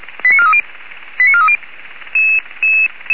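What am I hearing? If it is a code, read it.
Five-tone selective call (ZVEI), heard as quick runs of five tones, tones of about 70 ms. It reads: 89430 89430
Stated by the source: ZVEI five-tone selective call 89325 received on SDR.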